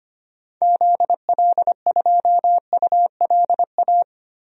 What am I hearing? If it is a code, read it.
Morse code, heard as a keyed tone, about 25 wpm, 700 Hz. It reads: ZL2ULA
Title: DXCC-100 Call Signs - 6 Characters - Repeated 2x - 25wpm